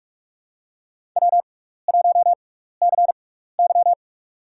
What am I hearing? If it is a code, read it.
Morse code, heard as a keyed tone, about 44 wpm, spaced as for 15 wpm, 700 Hz. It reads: W1CY
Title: USA Call Signs - 6x Character Spacing - 45wpm